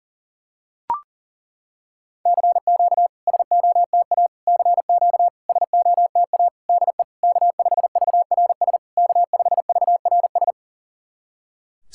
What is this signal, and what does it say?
Morse code, heard as a keyed tone, about 40 wpm, 700 Hz. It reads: CQ SOTA CQ SOTA DE K5VRS K5VRS
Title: SOTA QSO Practice 40wpm